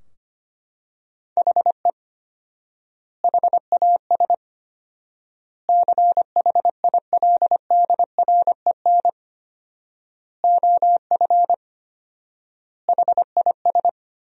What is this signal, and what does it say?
Morse code, heard as a keyed tone, about 25 wpm, 700 Hz. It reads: HE HAS CHILDREN OF HIS
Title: 4x Word-Spacing - Sentences from Top 300 Words - 25wpm